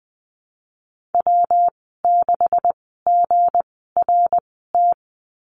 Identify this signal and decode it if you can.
Morse code, heard as a keyed tone, about 20 wpm, spaced as for 14 wpm, 700 Hz. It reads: W6GRT
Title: 219 Lesson 35g - Number 0 - Call Signs 20wpm